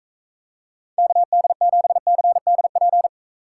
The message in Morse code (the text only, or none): KD7CDP